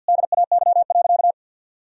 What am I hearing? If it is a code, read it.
Morse code, heard as a keyed tone, about 50 wpm, 700 Hz. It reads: DAY.
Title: Sentences from Top 1000 Words - Review Part 3 - 50wpm